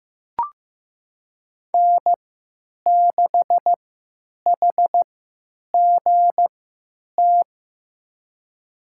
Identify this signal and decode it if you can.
Morse code, heard as a keyed tone, about 15 wpm, spaced as for 9 wpm, 700 Hz. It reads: N6HGT
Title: USA Call Signs - 5 Characters - 3x Character Spacing - 15wpm